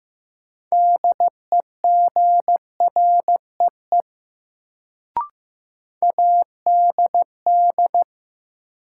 Morse code, heard as a keyed tone, about 15 wpm, 700 Hz. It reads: DEGREE ADD
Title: Sentences from Top 1500 Words - New Word Drill Alt - 15wpm